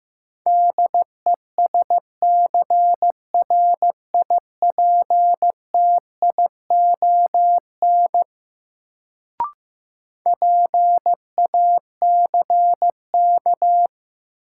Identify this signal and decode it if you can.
Morse code, heard as a keyed tone, about 15 wpm, 700 Hz. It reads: DESCRIPTION PACK
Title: Sentences from Top 1300 Words - New Word Drill - 15wpm